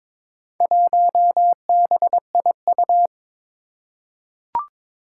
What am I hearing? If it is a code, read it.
Morse code, heard as a keyed tone, about 22 wpm, 700 Hz. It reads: 1BIU
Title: Four Challenging Letter-Number Combinations 22wpm